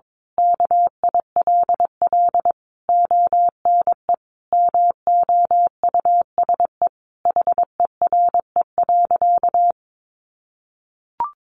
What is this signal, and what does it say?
Morse code, heard as a keyed tone, about 22 wpm, 700 Hz. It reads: KILL ONE MOUSE HERE.
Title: A Sound Of Thunder 22wpm